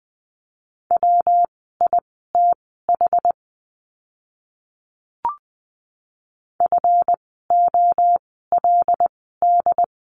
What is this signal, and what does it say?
Morse code, heard as a keyed tone, about 20 wpm, spaced as for 14 wpm, 700 Hz. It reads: WITH FOLD
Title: 139 Lesson 23f - Letter f - Words 20wpm